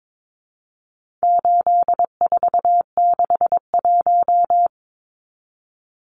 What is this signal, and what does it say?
Morse code, heard as a keyed tone, about 22 wpm, 700 Hz. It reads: 8461
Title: Numbers - 4-Digit 22wpm